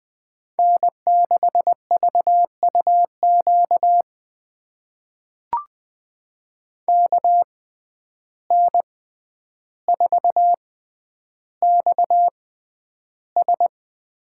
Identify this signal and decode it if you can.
Morse code, heard as a keyed tone, about 20 wpm, spaced as for 10 wpm, 700 Hz. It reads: N6VUQ KN4XS
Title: USA Call Signs - 5 Characters - 6x Character Spacing - 20wpm